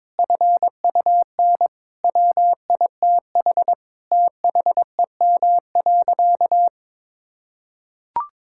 Morse code, heard as a keyed tone, about 22 wpm, 700 Hz. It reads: FUN WITH THEM.
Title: Sentences from Top 1000 Words - Review Part 5 - 22wpm